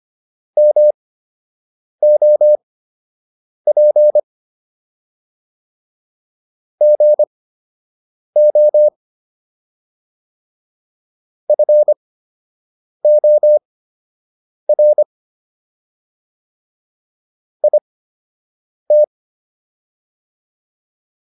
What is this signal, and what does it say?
Morse code, heard as a keyed tone, about 25 wpm, spaced as for 7 wpm, 600 Hz. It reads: MOP GO FOR IT